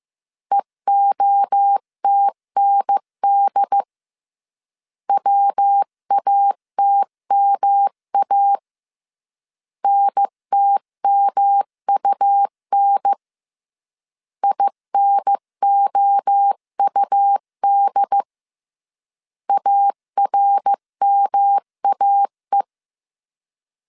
Morse code, heard as a keyed tone, about 15 wpm, 800 Hz. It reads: EOTND WATMA NTMUN INOUD ARMAE
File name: CW / DET / 1 3 1500C S A 14wpm 2447